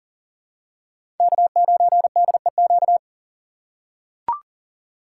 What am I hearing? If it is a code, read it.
Morse code, heard as a keyed tone, about 40 wpm, 700 Hz. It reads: K9DEQ
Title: USA Call Signs - 5 Characters - 6x Character Spacing - 40wpm